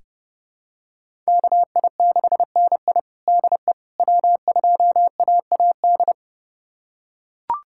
Morse code, heard as a keyed tone, about 30 wpm, 700 Hz. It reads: KI6NI DE W2AAD
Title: USA Callsign DE Callsign - 5 Characters - 30wpm